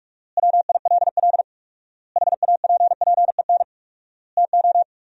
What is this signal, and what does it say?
Morse code, heard as a keyed tone, about 46 wpm, 700 Hz. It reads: WILL HAPPEN TO
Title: 4x Word-Spacing - Sentences from Top 500 Words - 45wpm